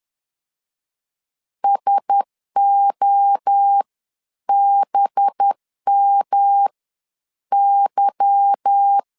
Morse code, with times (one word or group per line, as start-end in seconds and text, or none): 1.6-6.7 SOBM
7.5-9.0 Y